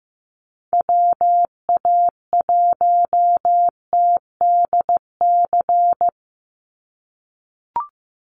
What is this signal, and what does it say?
Morse code, heard as a keyed tone, about 15 wpm, 700 Hz. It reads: WA1TDC
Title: USA Call Signs 15wpm